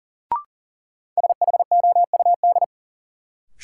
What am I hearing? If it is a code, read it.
Morse code, heard as a keyed tone, about 39 wpm, 700 Hz. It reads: SHOUD
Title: Sentences from Top 1000 Words - Review Part 4 - New Word Drill - 40wpm